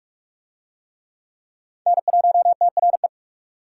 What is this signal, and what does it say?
Morse code, heard as a keyed tone, about 44 wpm, 700 Hz. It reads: N1TRE